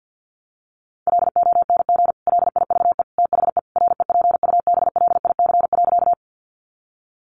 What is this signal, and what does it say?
Morse code, heard as a keyed tone, about 50 wpm, 700 Hz. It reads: LONG LIVE THE REPUBLIC.
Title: Sentences from Top 1500 Words 50wpm